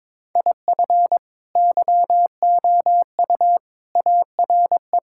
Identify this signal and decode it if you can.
Morse code, heard as a keyed tone, about 22 wpm, 700 Hz. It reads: IF YOU ARE
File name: practice / 22wpm / Sets of 6 Words - 22wpm